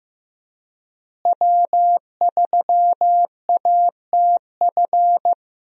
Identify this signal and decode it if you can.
Morse code, heard as a keyed tone, about 16 wpm, 700 Hz. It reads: W3ATF
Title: USA Call Signs - 5 Characters - Repeated 2x - 15wpm